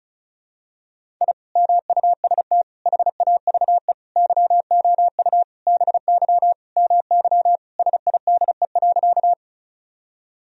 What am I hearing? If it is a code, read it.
Morse code, heard as a keyed tone, about 36 wpm, 700 Hz. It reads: I MUST HAVE YOU BY MY SIDE.
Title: Sentences from Top 1000 Words - Review Part 1 - 35wpm